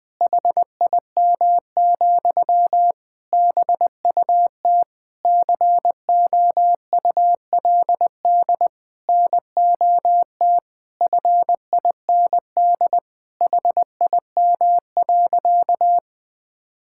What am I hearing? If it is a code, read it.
Morse code, heard as a keyed tone, about 20 wpm, 700 Hz. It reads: HIM, BUT COULD NOT FIND HIM.
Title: Sentences from Top 1000 Words - Review Part 5 - 20wpm